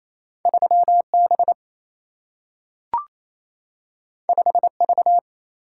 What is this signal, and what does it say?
Morse code, heard as a keyed tone, about 28 wpm, 700 Hz. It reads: 3 B 5 V
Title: Single-Letter Single-Number 28wpm